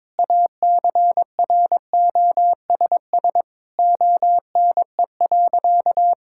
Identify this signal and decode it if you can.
Morse code, heard as a keyed tone, about 22 wpm, 700 Hz. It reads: ACROSS ONE.
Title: Sentences from Top 1000 Words - Review Part 5 - 22wpm